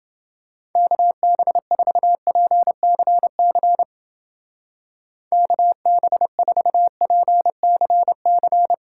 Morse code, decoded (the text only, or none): KB4PCC KB4PCC